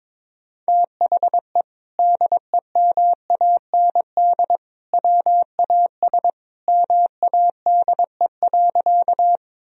Morse code, heard as a keyed tone, about 22 wpm, 700 Hz. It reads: THE DEMAND WAS MADE.
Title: Sentences from Top 1000 Words - Review Part 4 - 22wpm